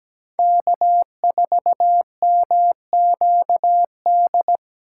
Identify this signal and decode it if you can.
Morse code, heard as a keyed tone, about 20 wpm, 700 Hz. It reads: K4MQD
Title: Speed Racing - USA 1x3 Call Signs 17wpm